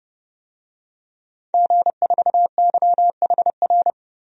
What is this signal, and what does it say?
Morse code, heard as a keyed tone, about 30 wpm, 700 Hz. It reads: G4YHR
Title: DXCC-100 Call Signs - 5 Characters - 30wpm